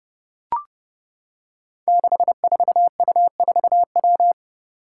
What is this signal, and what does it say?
Morse code, heard as a keyed tone, about 30 wpm, 700 Hz. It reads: 64U4W
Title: Five Challenging Letter-Number Combinations 30wpm